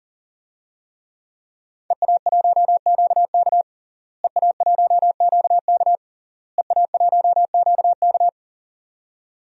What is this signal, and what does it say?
Morse code, heard as a keyed tone, about 40 wpm, 700 Hz. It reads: EA1QK EA1QK EA1QK
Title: DXCC-100 Call Signs - 5 Characters - Repeated 3x - 40wpm